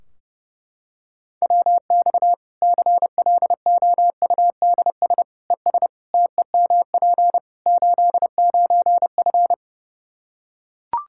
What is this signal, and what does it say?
Morse code, heard as a keyed tone, about 30 wpm, 700 Hz. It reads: WX CLOUDS ES TEMP 89F
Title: Practice QSO - Exchange 2 - Info - 30wpm